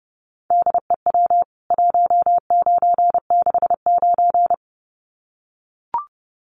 Morse code, heard as a keyed tone, about 30 wpm, 700 Hz. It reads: DEW 1969
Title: CWT Exchange 30wpm